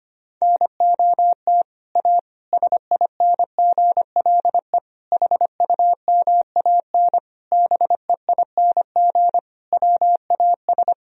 Morse code, heard as a keyed tone, about 25 wpm, 700 Hz. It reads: NOT A SINGLE HUMAN BEING WAS
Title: Sentences from Top 1000 Words - Review Part 4 - 25wpm